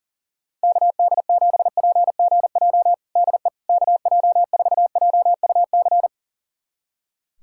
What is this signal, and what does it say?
Morse code, heard as a keyed tone, about 40 wpm, 700 Hz. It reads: KD7PGJ DE KJ4JUC
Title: USA Callsign DE Callsign 40wpm